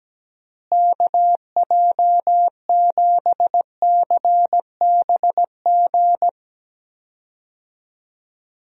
Morse code, heard as a keyed tone, about 17 wpm, 700 Hz. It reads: KJ7CBG
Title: USA Call Signs - 6 Characters - Repeated 2x - 17wpm